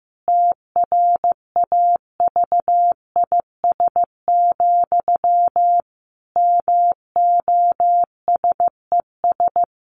Morse code, heard as a keyed tone, about 15 wpm, 700 Hz. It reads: TRAVIS, MOSES
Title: A Sound Of Thunder 15wpm